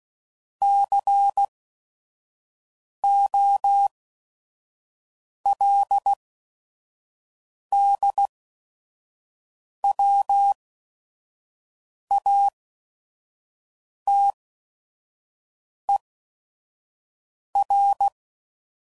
Morse code, heard as a keyed tone, about 16 wpm, spaced as for 5 wpm, 800 Hz. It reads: COLDWATER
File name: CW Code Groups @5WPM / fromSM's / Cities